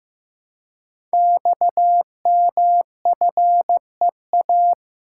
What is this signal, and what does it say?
Morse code, heard as a keyed tone, about 15 wpm, 700 Hz. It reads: XMFEA
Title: Code Groups - 5 Letter:Numbers - 15wpm